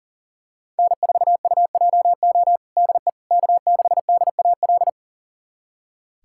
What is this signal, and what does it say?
Morse code, heard as a keyed tone, about 37 wpm, 700 Hz. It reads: N4UJO DE K6DAL
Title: USA Callsign DE Callsign - 5 Characters - 40wpm